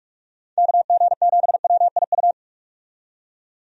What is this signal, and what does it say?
Morse code, heard as a keyed tone, about 41 wpm, 700 Hz. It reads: KG7WIU
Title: USA Call Signs - 6 Characters - 45wpm